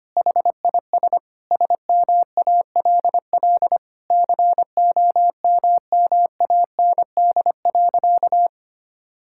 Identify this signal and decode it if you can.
Morse code, heard as a keyed tone, about 25 wpm, 700 Hz. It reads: HIS SMALL COMMAND.
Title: Sentences from Top 1600 Words 25wpm